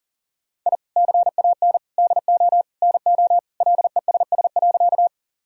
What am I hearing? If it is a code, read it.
Morse code, heard as a keyed tone, about 40 wpm, 700 Hz. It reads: I CAN DO NO LESS.
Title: Sentences from Top 1000 Words - Review Part 2 - 40wpm